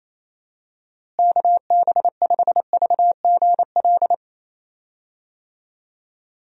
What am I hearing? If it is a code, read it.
Morse code, heard as a keyed tone, about 28 wpm, 700 Hz. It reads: KB5VGL
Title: USA Call Signs - 6 Characters - Repeated 3x - 28wpm